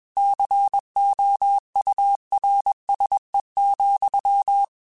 Morse code, heard as a keyed tone, about 21 wpm, 800 Hz. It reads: COURSE,
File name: QSOs / QSOs-mp3s / WA7QOX 1@20